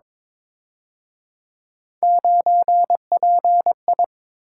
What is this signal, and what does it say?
Morse code, heard as a keyed tone, about 26 wpm, 700 Hz. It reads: 9PI
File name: Speed-Racing - Three Letter-Number Combinations - 22wpm